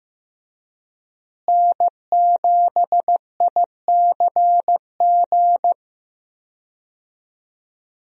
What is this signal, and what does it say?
Morse code, heard as a keyed tone, about 15 wpm, 700 Hz. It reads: N7ICG
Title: USA Call Signs - 5 Characters - Repeated 2x - 15wpm